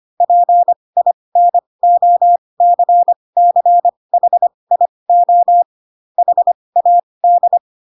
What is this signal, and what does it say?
Morse code, heard as a keyed tone, about 25 wpm, spaced as for 20 wpm, 700 Hz. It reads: PINOCCHIO HAD